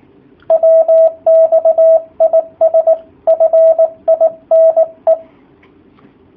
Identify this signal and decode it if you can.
Morse code, heard as a keyed tone, about 18 wpm, 650 Hz, with radio noise. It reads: WX IS FINE
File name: jh8cbh / 2019cw / sp18- w